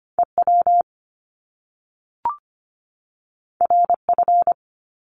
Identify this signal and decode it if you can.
Morse code, heard as a keyed tone, about 25 wpm, 700 Hz. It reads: EW RF
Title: Top Two Letter Combinations in Words 25wpm